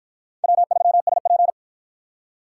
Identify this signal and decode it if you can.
Morse code, heard as a keyed tone, about 50 wpm, 700 Hz. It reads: W3SP